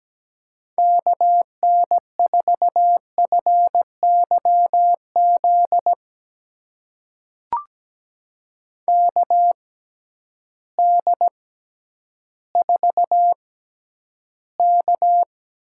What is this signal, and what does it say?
Morse code, heard as a keyed tone, about 17 wpm, spaced as for 10 wpm, 700 Hz. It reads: KN4FYZ KD4K
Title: USA Call Signs - 6 Characters - 6x Character Spacing - 17wpm